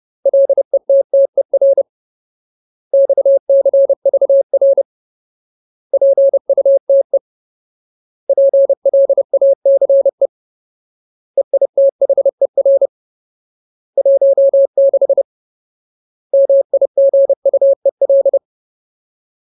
Morse code, heard as a keyed tone, about 30 wpm, 550 Hz. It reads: LETTER XCVR PUTE PLACE EITHER 16 MIGUEL
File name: Ultimate Word List / 30WPM Ultimate